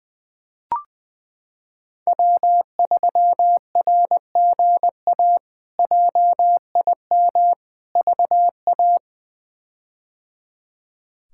Morse code, heard as a keyed tone, about 20 wpm, 700 Hz. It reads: W3RGA JIM VA
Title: K1USN SST QSO Practice - 20wpm